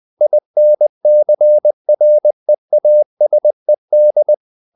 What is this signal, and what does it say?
Morse code, heard as a keyed tone, about 20 wpm, 600 Hz. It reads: INCREASED
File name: the old man and the sea / the old man and the sea part 1 20wpm